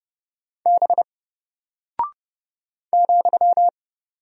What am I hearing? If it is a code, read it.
Morse code, heard as a keyed tone, about 30 wpm, 700 Hz. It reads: B ,
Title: Single Character (Letter:Number:Punctuation:Prosign) Alt 30wpm